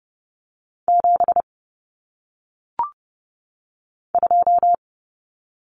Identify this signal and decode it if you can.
Morse code, heard as a keyed tone, about 30 wpm, 700 Hz. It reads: 7 2
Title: Common Cut Numbers - 1-Digit 30wpm